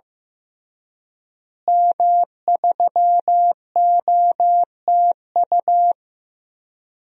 Morse code, heard as a keyed tone, about 15 wpm, 700 Hz. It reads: M3OTU
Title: DXCC-100 Call Signs - 5 Characters - Repeated 3x - 15wpm